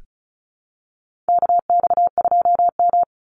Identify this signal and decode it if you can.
Morse code, heard as a keyed tone, about 34 wpm, 700 Hz. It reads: KX2M
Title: USA Call Signs 35wpm